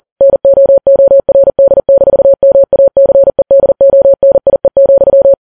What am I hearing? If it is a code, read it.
Morse code, heard as a keyed tone, about 40 wpm, 550 Hz. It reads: NOORD-MACEDONIE,
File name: Files Web tls / Morse on9fv / morse40kps